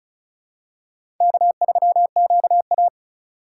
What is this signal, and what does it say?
Morse code, heard as a keyed tone, about 35 wpm, 700 Hz. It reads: K3QA
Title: USA Call Signs - 4 Characters - 2x Character Spacing - 35wpm